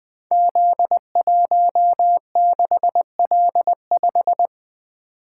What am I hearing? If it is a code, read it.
Morse code, heard as a keyed tone, about 20 wpm, 700 Hz. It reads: Z16L5
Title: Code Groups - 5 Letter:Numbers - 20wpm